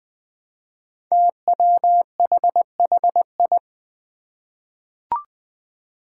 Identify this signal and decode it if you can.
Morse code, heard as a keyed tone, about 20 wpm, 700 Hz. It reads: TWHHI
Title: Five Challenging Letter-Number Combinations 20wpm